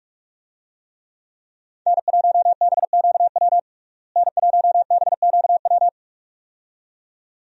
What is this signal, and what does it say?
Morse code, heard as a keyed tone, about 41 wpm, 700 Hz. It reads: N1BQW N1BQW
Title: USA Call Signs - 5 Characters - Repeated 2x - 45wpm